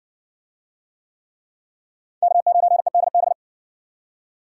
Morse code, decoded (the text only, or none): K9EDB